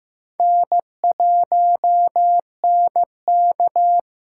Morse code, heard as a keyed tone, about 15 wpm, 700 Hz. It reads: N1NK
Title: USA Call Signs - 6x Character Spacing - 15wpm